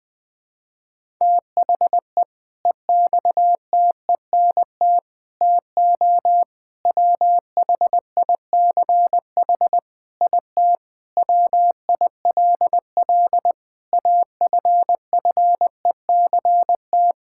Morse code, heard as a keyed tone, about 20 wpm, 700 Hz. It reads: THE EXTENT TO WHICH IT WILL AFFECT
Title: Sentences with 4-Word Phrases - Spoken Beginning - 20wpm